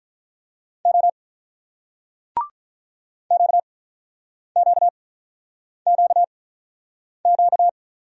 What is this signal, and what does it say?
Morse code, heard as a keyed tone, about 44 wpm, 700 Hz. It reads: O Q Q Q Q